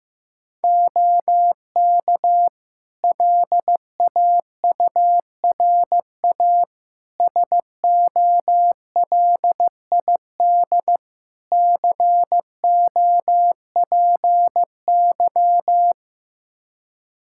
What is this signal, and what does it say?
Morse code, heard as a keyed tone, about 15 wpm, 700 Hz. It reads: OK LAURA SOLID COPY
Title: K1USN SST QSO Practice 15wpm